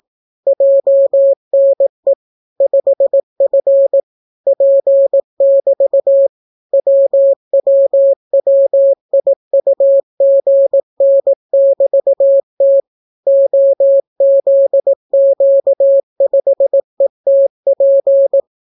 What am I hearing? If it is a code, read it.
Morse code, heard as a keyed tone, about 18 wpm, 550 Hz. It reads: JNE 5F P<BT> WWWIUGN<BT>T OZQ5ETP